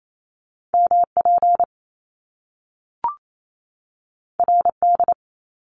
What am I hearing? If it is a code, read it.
Morse code, heard as a keyed tone, about 28 wpm, 700 Hz. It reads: MP RD
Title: Top Two Letter Combinations in Words 28wpm